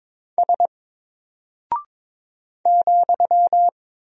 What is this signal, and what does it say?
Morse code, heard as a keyed tone, about 22 wpm, 700 Hz. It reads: S ,
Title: Single Character (Letter:Number:Punctuation:Prosign) 22wpm